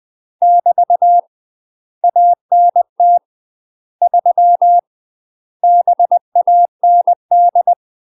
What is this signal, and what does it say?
Morse code, heard as a keyed tone, about 20 wpm, 700 Hz. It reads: <BT> ANT 3 BAND
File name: MASTER QSOs TRAINING/20WPM